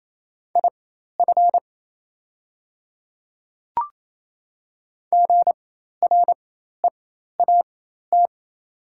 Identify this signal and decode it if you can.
Morse code, heard as a keyed tone, about 28 wpm, spaced as for 13 wpm, 690 Hz. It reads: IF GREAT
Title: 4x Character Spacing - Top 300 Words - 28wpm